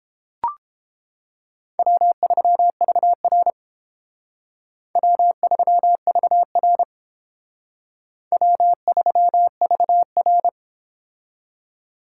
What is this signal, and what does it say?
Morse code, heard as a keyed tone, about 30 wpm, 700 Hz. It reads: W3VR W3VR W3VR